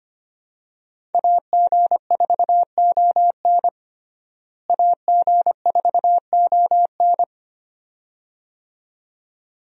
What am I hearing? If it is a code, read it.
Morse code, heard as a keyed tone, about 25 wpm, 700 Hz. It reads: AG4ON AG4ON
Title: USA Call Signs - 5 Characters - Repeated 2x - 25wpm